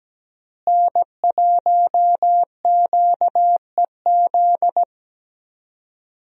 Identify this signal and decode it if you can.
Morse code, heard as a keyed tone, about 17 wpm, 700 Hz. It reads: N1QEZ
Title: USA Call Signs - 5 Characters - Repeated 3x - 17wpm